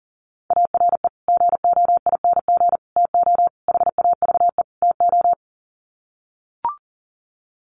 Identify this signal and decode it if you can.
Morse code, heard as a keyed tone, about 40 wpm, 700 Hz. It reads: ARE GOING TO HAVE TO
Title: Sets of 5 Words 40wpm